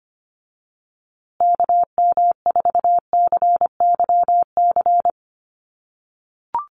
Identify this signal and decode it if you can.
Morse code, heard as a keyed tone, about 25 wpm, 700 Hz. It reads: KM4CYC
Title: USA Call Signs Repeated 3x 25wpm